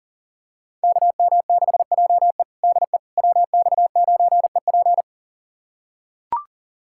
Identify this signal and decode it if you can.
Morse code, heard as a keyed tone, about 40 wpm, 700 Hz. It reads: KM6JE DE WX9EP
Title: USA Callsign DE Callsign 40wpm